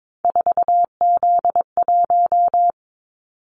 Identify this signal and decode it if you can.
Morse code, heard as a keyed tone, about 22 wpm, 700 Hz. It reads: 4Z1
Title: Callsign Prefixes for Top 50 DXCC - 22wpm